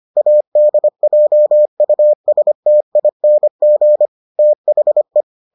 Morse code, heard as a keyed tone, about 25 wpm, 600 Hz. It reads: ADJUSTING THE